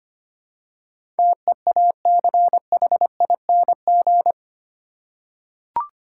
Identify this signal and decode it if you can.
Morse code, heard as a keyed tone, about 25 wpm, 700 Hz. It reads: TEACHING
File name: Eight Letter Words - 25wpm